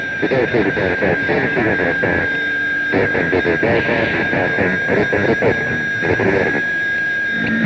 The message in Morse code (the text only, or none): none